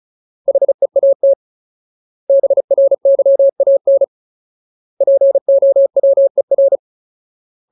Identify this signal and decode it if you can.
Morse code, heard as a keyed tone, about 34 wpm, 550 Hz. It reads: HEAT BRYAN POWER
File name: Ultimate Word List / 35WPM Ultimate